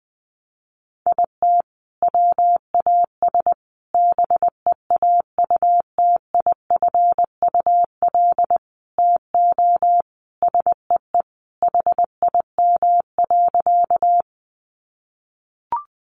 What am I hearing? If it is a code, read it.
Morse code, heard as a keyed tone, about 19 wpm, 700 Hz. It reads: IT WAS BEAUTIFUL TO SEE HIM.
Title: Sentences from Top 900 Words 20wpm